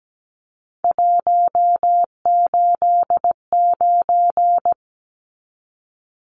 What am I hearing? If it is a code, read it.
Morse code, heard as a keyed tone, about 17 wpm, 700 Hz. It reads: 189
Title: Numbers - 3-Digit 17wpm